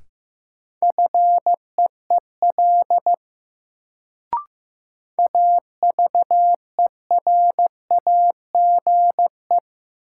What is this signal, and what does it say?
Morse code, heard as a keyed tone, about 15 wpm, 700 Hz. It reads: FEEL AVERAGE
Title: Sets of 5 Words - New Word Drill - 15wpm